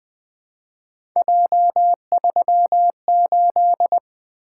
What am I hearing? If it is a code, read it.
Morse code, heard as a keyed tone, about 20 wpm, 700 Hz. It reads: J38
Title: Three Letter-Number Combinations - 20wpm